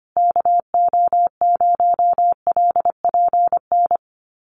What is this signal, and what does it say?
Morse code, heard as a keyed tone, about 25 wpm, 700 Hz. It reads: KO0LPN
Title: USA Call Signs Repeated 3x 25wpm